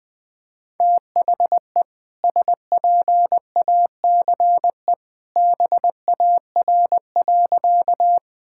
Morse code, heard as a keyed tone, about 21 wpm, 700 Hz. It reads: THE SPACE BAR.
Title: All-Age Appropriate Jokes - 20wpm